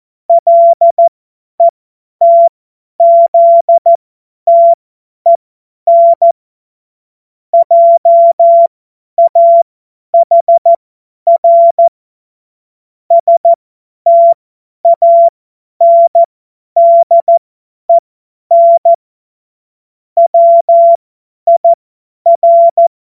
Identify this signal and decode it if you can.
Morse code, heard as a keyed tone, about 14 wpm, spaced as for 10 wpm, 680 Hz. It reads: LETZTEN JAHR STANDEN WIR
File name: Morsekurs / Witze / Witz 17 Juni 2025 50 F70